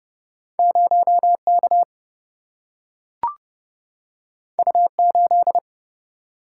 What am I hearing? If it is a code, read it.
Morse code, heard as a keyed tone, about 30 wpm, 700 Hz. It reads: 0K U8
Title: Single-Letter Single-Number 30wpm